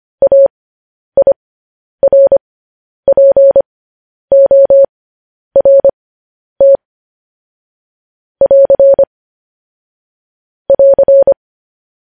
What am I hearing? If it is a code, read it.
Morse code, heard as a keyed tone, about 25 wpm, spaced as for 10 wpm, 550 Hz. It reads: AIRPORT <AR> <AR>